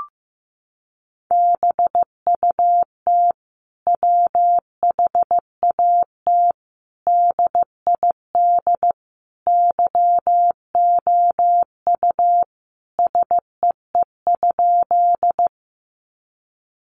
Morse code, heard as a keyed tone, about 14 wpm, 700 Hz. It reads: BUT WHAT DID YOU SEE?
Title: Sentences from Top 100 Words 15wpm